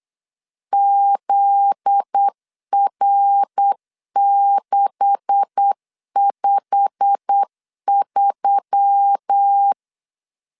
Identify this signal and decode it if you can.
Morse code, heard as a keyed tone, about 8 wpm, 800 Hz. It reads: ZR653
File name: LEZIONI / 9 286 8wpm 3425